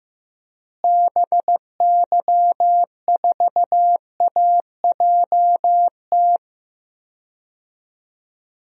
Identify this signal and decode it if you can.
Morse code, heard as a keyed tone, about 15 wpm, 700 Hz. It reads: BY4AJT
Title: DXCC-100 Call Signs - 6 Characters - Repeated 2x - 15wpm